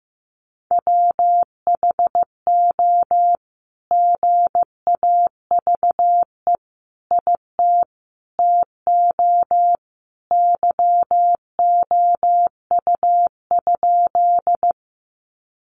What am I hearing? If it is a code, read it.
Morse code, heard as a keyed tone, about 15 wpm, 700 Hz. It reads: WHO GAVE IT TO YOU?
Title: Sentences from Top 500 Words 15wpm